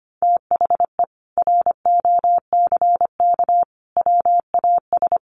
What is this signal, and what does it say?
Morse code, heard as a keyed tone, about 25 wpm, 700 Hz. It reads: THE ROCK WAS
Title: Sentences from Top 400 Words 25wpm